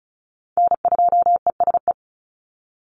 Morse code, heard as a keyed tone, about 34 wpm, 700 Hz. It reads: N2ESE
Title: NA SOTA Chaser Callsigns 35wpm